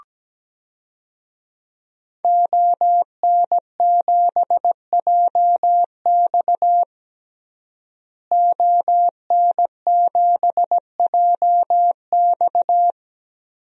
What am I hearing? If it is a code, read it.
Morse code, heard as a keyed tone, about 17 wpm, 700 Hz. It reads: ON7JX ON7JX
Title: DXCC-100 Call Signs - 5 Characters - Repeated 3x - 17wpm